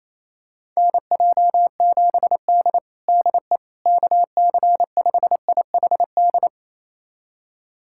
Morse code, heard as a keyed tone, about 28 wpm, 700 Hz. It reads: NJ7D DE KC5IHD
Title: USA Callsign DE Callsign 28wpm